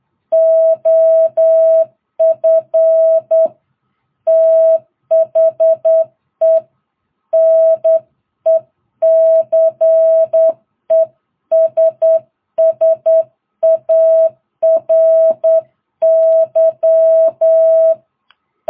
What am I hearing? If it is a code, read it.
Morse code, heard as a keyed tone, about 9 wpm, 660 Hz. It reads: OF THE NECESSARY